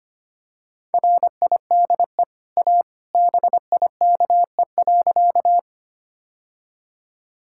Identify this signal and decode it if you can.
Morse code, heard as a keyed tone, about 25 wpm, 690 Hz. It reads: RIDE A BIKE.
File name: Easy Sentences with Hints - 25wpm